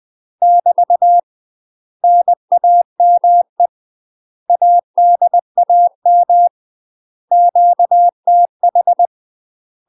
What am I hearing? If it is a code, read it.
Morse code, heard as a keyed tone, about 20 wpm, 700 Hz. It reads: <BT> NAME ADAM QTH
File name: UK MASTER QSOs TRAINING / 20WPM / UK MASTER QSOs 0025